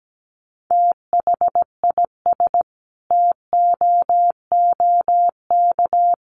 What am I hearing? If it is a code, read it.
Morse code, heard as a keyed tone, about 17 wpm, 700 Hz. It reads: THIS TOOK
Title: Sentences from Top 300 Words 17wpm